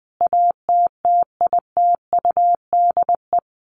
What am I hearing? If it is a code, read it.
Morse code, heard as a keyed tone, about 19 wpm, 700 Hz. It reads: ATTITUDE
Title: Sentences from Top 1300 Words 20wpm